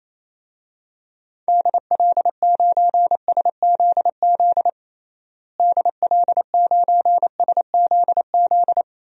Morse code, heard as a keyed tone, about 28 wpm, 700 Hz. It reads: DL9SZZ DL9SZZ
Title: DXCC-100 Call Signs - 6 Characters - Repeated 2x - 28wpm